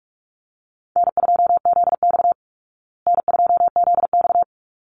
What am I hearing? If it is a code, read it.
Morse code, heard as a keyed tone, about 44 wpm, 700 Hz. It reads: N2ZX N2ZX
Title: Easy Call Signs Repeated 2x - 45wpm